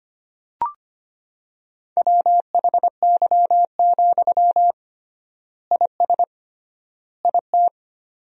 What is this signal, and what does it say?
Morse code, heard as a keyed tone, about 25 wpm, 700 Hz. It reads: WHY, IS IT
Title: 3x Word-Spacing - Sentences from Top 300 Words - 25wpm